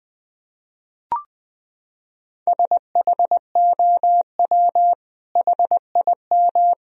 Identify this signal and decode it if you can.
Morse code, heard as a keyed tone, about 21 wpm, 700 Hz. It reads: SHOW HIM
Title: Sentences from Top 1000 Words - Review Part 1 - 20wpm